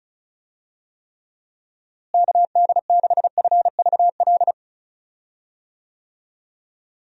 Morse code, heard as a keyed tone, about 36 wpm, 700 Hz. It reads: KD6FVL
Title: USA Call Signs - 6 Characters - Repeated 2x - 35wpm